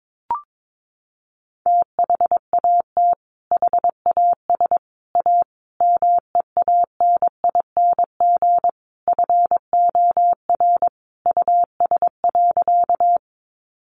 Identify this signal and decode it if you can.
Morse code, heard as a keyed tone, about 22 wpm, 700 Hz. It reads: THAT HAS A MEANING FOR US.
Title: Sentences from Top 900 Words 22wpm